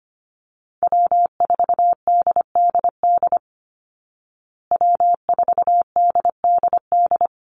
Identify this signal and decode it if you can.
Morse code, heard as a keyed tone, about 25 wpm, 700 Hz. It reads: W4DDD W4DDD
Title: Call Signs Repeated 3x - 25wpm